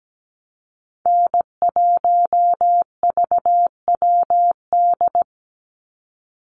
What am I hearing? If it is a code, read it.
Morse code, heard as a keyed tone, about 17 wpm, 700 Hz. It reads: N1VWD
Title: NA SOTA Chaser Callsigns 17wpm